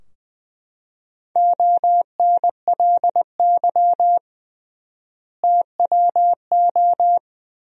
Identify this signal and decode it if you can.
Morse code, heard as a keyed tone, about 20 wpm, 700 Hz. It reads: ONLY TWO
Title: 3x Word-Spacing - Sentences from Top 200 Words - 20wpm